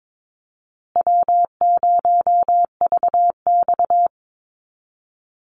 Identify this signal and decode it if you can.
Morse code, heard as a keyed tone, about 22 wpm, 700 Hz. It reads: W0VX
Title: Easy Call Signs Repeated 3x - 22wpm